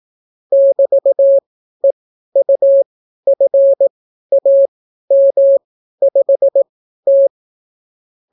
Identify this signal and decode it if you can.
Morse code, heard as a keyed tone, about 18 wpm, spaced as for 12 wpm, 550 Hz. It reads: <BT>EUFAM5T